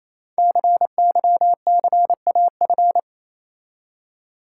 Morse code, heard as a keyed tone, about 28 wpm, 700 Hz. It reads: CYCAF
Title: Code Groups - 5 Letters - 28wpm